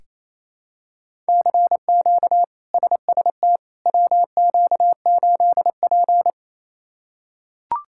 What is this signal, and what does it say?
Morse code, heard as a keyed tone, about 28 wpm, 700 Hz. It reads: CQ SST WQ8P
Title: K1USN SST QSO Practice - 28wpm